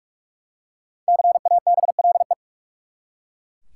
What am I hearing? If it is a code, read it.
Morse code, heard as a keyed tone, about 42 wpm, 700 Hz. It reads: CABLE
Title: Five Letter Words - 45wpm